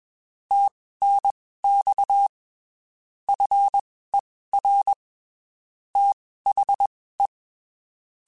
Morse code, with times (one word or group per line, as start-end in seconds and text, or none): 0.5-2.3 TNX
3.3-4.9 FER
6.0-7.3 THE